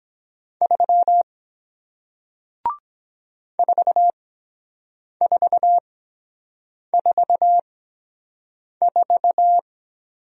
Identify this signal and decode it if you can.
Morse code, heard as a keyed tone, about 23 wpm, spaced as for 7 wpm, 700 Hz. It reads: 3 4444